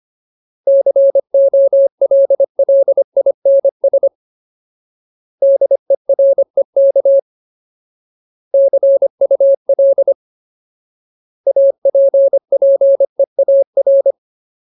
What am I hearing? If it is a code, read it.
Morse code, heard as a keyed tone, about 25 wpm, 550 Hz. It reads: COLLINS DEREK CUL APPEAR
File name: Ultimate Word List / 25WPM Ultimate